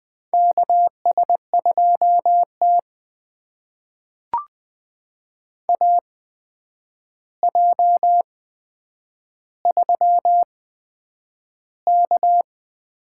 Morse code, heard as a keyed tone, about 20 wpm, spaced as for 9 wpm, 700 Hz. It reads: KS2T AJ3K
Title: USA Call Signs - 4 Characters - 8x Character Spacing - 20wpm